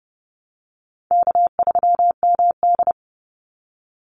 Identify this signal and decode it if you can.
Morse code, heard as a keyed tone, about 30 wpm, 700 Hz. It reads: K3MD
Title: Easy Call Signs Repeated 3x - 30wpm